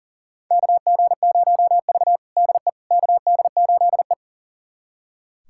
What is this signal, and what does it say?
Morse code, heard as a keyed tone, about 40 wpm, 700 Hz. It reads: KG0V DE KD8E
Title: USA Callsign DE Callsign - 4 Characters - 40wpm